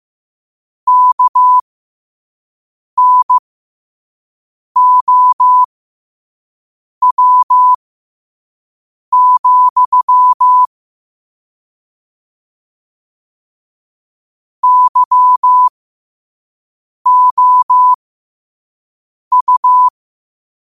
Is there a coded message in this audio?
Yes: Morse code from 0.9 to 19.9 s, heard as a keyed tone, about 15 wpm, spaced as for 5 wpm, 1000 Hz. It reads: KNOW, YOU